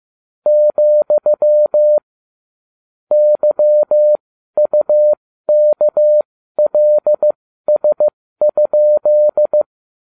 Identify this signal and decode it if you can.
Morse code, heard as a keyed tone, about 15 wpm, spaced as for 12 wpm, 600 Hz. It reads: , YUKLS?